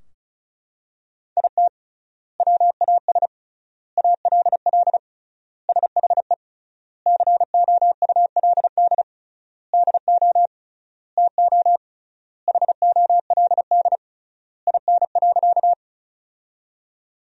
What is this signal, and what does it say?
Morse code, heard as a keyed tone, about 36 wpm, 700 Hz. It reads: IT WAS ALL SHE COULD DO TO HOLD IN.
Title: 3x Word-Spacing - Sentences from Top 400 Words - 35wpm